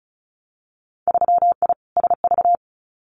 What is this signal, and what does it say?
Morse code, heard as a keyed tone, about 36 wpm, 700 Hz. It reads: 3I SV
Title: Winter Field Day Exchanges Encore 35wpm